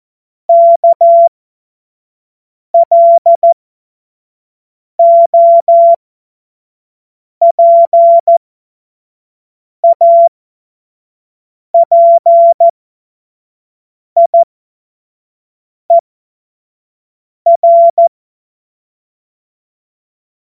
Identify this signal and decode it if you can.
Morse code, heard as a keyed tone, about 14 wpm, spaced as for 5 wpm, 680 Hz. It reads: KLOPAPIER